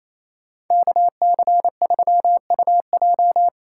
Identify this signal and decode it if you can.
Morse code, heard as a keyed tone, about 28 wpm, 700 Hz. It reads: KC3UJ
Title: Speed Racing - USA 2x2 Call Signs 28wpm